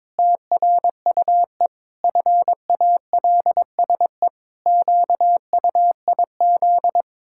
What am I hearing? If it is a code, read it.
Morse code, heard as a keyed tone, about 22 wpm, 700 Hz. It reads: TRUE FALSE QUIZ
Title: Pangram Challenge v2 - 22wpm